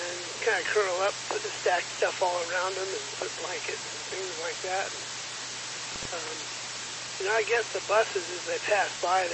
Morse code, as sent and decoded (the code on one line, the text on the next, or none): none